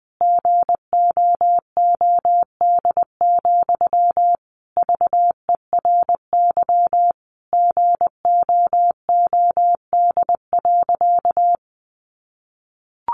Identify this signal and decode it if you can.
Morse code, heard as a keyed tone, about 20 wpm, 700 Hz. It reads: GOOD, VERY GOOD.
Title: Sentences from Top 200 Words 20wpm